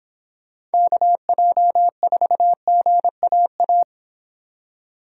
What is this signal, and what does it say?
Morse code, heard as a keyed tone, about 25 wpm, 700 Hz. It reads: KJ4GAA